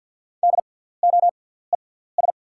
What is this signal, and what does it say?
Morse code, heard as a keyed tone, about 48 wpm, spaced as for 17 wpm, 690 Hz. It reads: DOES
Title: 6x Character Spacing - Top 200 Words - 50wpm